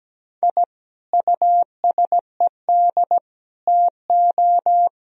Morse code, heard as a keyed tone, about 17 wpm, 700 Hz. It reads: I USED TO
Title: Sentences from Top 1000 Words - Review Part 4 - 17wpm